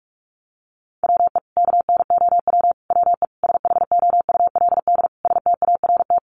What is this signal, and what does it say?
Morse code, heard as a keyed tone, about 44 wpm, 700 Hz. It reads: WE KNOW WE SHOULD START
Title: Sentences from Top 1100 Words 45wpm